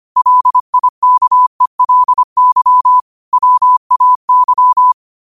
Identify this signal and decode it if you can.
Morse code, heard as a keyed tone, about 25 wpm, 1000 Hz. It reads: LIKELY WAY
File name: qotdfiles / 25wpm